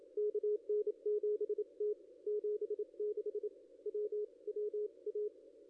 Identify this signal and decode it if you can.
Morse code, heard as a keyed tone, about 28 wpm, spaced as for 25 wpm, 410 Hz. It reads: KN7T 76 WWA
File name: contests / ss-cw-k5tr-op-n5rz -0500